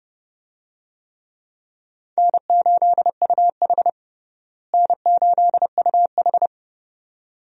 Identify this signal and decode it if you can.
Morse code, heard as a keyed tone, about 29 wpm, 700 Hz. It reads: N8UH N8UH
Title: USA Call Signs - 4 Characters - Repeated 2x - 30wpm